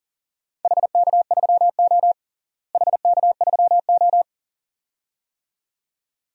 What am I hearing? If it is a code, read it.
Morse code, heard as a keyed tone, about 40 wpm, 700 Hz. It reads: HK3O HK3O
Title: DXCC-100 Call Signs - 4 Characters - Repeated 2x - 40wpm